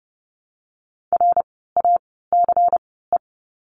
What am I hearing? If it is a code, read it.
Morse code, heard as a keyed tone, about 30 wpm, 700 Hz. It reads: R A C E
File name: Lesson 16f - Letter c - Words 30wpm